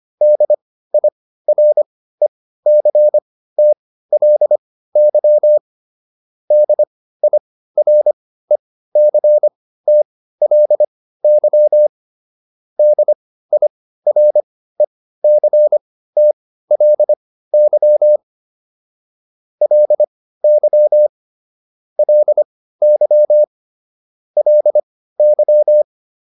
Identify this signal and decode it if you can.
Morse code, heard as a keyed tone, about 25 wpm, spaced as for 15 wpm, 600 Hz. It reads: DIRECTLY DIRECTLY DIRECTLY LY LY LY